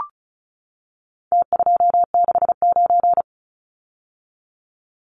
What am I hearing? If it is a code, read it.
Morse code, heard as a keyed tone, about 34 wpm, 700 Hz. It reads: T269
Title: Numbers - Contest Style - 4-Digit 35wpm